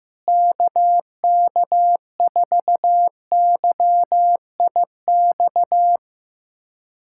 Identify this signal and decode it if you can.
Morse code, heard as a keyed tone, about 15 wpm, 700 Hz. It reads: KK4YIX
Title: USA Callsign DE Callsign - 6 Characters - 15wpm